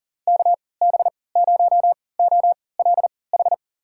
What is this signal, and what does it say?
Morse code, heard as a keyed tone, about 37 wpm, spaced as for 22 wpm, 700 Hz. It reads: KB0OLH
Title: USA Call Signs - 6 Characters - 3x Character Spacing - 40wpm